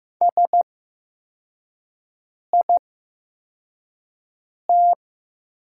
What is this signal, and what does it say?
Morse code, heard as a keyed tone, about 15 wpm, spaced as for 4 wpm, 700 Hz. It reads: SIT
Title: 8x Character Spacing - Top 500 Words - 15wpm